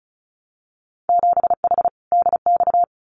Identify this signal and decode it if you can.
Morse code, heard as a keyed tone, about 36 wpm, 700 Hz. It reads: 7H DX
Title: Winter Field Day Exchanges Encore 35wpm